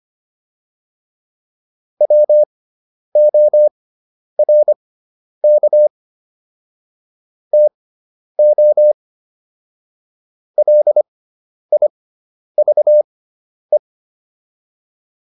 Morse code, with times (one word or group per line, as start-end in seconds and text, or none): 2.0-5.9 WORK
7.5-8.9 TO
10.6-13.8 LIVE